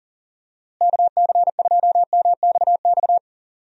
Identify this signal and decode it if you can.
Morse code, heard as a keyed tone, about 40 wpm, 700 Hz. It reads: KC2MXX
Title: USA Call Signs - 6 Characters - Repeated 3x - 40wpm